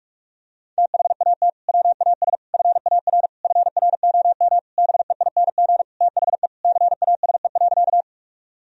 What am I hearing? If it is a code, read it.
Morse code, heard as a keyed tone, about 47 wpm, 700 Hz. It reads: THAT WAS FAR FROM BEING THE CASE.
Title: Sentences from Top 1000 Words - Review Part 3 - 45wpm